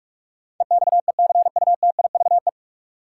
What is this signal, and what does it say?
Morse code, heard as a keyed tone, about 41 wpm, 700 Hz. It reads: EXECUTIVE
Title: Sets of 5 Words Encore - New Word Drill - 45wpm